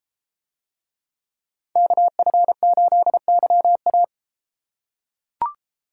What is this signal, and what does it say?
Morse code, heard as a keyed tone, about 32 wpm, 700 Hz. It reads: KF8YA